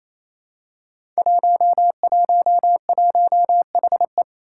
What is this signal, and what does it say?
Morse code, heard as a keyed tone, about 27 wpm, 700 Hz. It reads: 111HE
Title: Five Challenging Letter-Number Combinations 28wpm